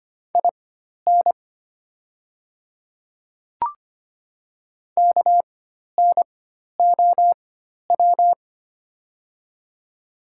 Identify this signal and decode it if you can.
Morse code, heard as a keyed tone, about 25 wpm, spaced as for 12 wpm, 690 Hz. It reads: IN KNOW
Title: 4x Character Spacing - Top 100 Words - 25wpm